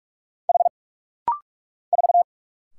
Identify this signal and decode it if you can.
Morse code, heard as a keyed tone, about 44 wpm, 700 Hz. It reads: H 4
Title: Single Letter-Number - Rapid-Fire 45wpm